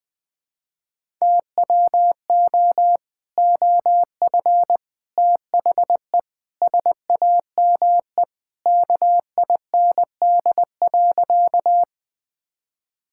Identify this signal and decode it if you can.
Morse code, heard as a keyed tone, about 21 wpm, 700 Hz. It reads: TWO OF THE SAME KIND.
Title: Sentences with 3-Word Phrases - Spoken Beginning - 20wpm